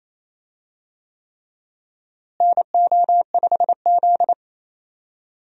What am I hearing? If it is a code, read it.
Morse code, heard as a keyed tone, about 28 wpm, 700 Hz. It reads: NO5Z
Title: USA Call Signs - 4 Characters - Repeated 3x - 28wpm